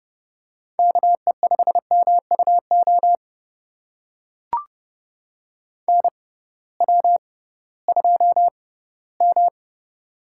Morse code, heard as a keyed tone, about 30 wpm, spaced as for 17 wpm, 700 Hz. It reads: KE5MUO NW2M
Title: USA Call Signs - 6x Character Spacing - 30wpm